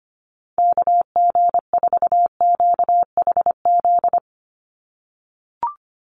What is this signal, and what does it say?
Morse code, heard as a keyed tone, about 25 wpm, 700 Hz. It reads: KG4QHZ
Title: USA Call Signs 25wpm